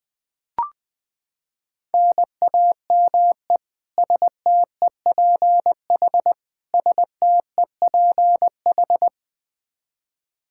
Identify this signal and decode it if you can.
Morse code, heard as a keyed tone, about 20 wpm, 700 Hz. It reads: NAME STEPH STEPH
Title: Practice QSO - Exchange 1 - Beginning - 20wpm